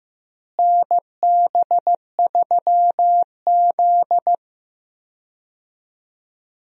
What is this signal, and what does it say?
Morse code, heard as a keyed tone, about 15 wpm, 700 Hz. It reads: NB3Z
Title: USA Call Signs - 4 Characters - Repeated 3x - 15wpm